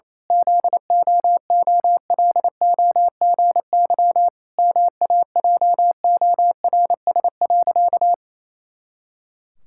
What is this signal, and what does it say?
Morse code, heard as a keyed tone, about 28 wpm, 700 Hz. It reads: ZOOLOGY MAJORS.
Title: Pangram Challenge v2 - 28wpm